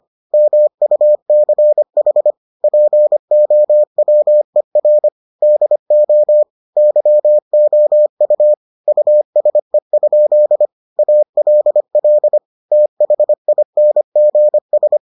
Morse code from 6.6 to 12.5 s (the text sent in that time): YOU USE? ALL